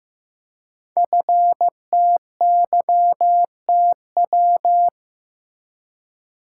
Morse code, heard as a keyed tone, about 15 wpm, 700 Hz. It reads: FTYTW
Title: Code Groups - 5 Letters - 15wpm